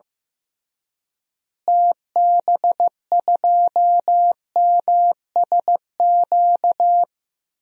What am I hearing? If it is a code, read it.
Morse code, heard as a keyed tone, about 15 wpm, 700 Hz. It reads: TB2MSQ
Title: DXCC-100 Call Signs - 6 Characters - Repeated 3x - 15wpm